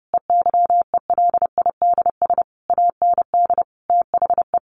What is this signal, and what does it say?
Morse code, heard as a keyed tone, about 30 wpm, 700 Hz. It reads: EYELIDS AND THE
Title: A Sound Of Thunder 30wpm